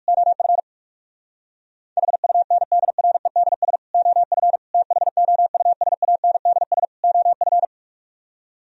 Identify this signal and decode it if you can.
Morse code, heard as a keyed tone, about 65 wpm, spaced as for 55 wpm, 700 Hz. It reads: OF HUNDREDS OF THOUSANDS OF
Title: Speed-Racing - Sets of 4 Words 45wpm